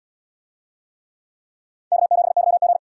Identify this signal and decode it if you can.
Morse code, heard as a keyed tone, about 74 wpm, 700 Hz. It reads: K7YG